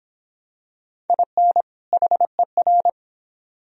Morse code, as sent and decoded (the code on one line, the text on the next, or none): .. -. / .... .-.
IN HER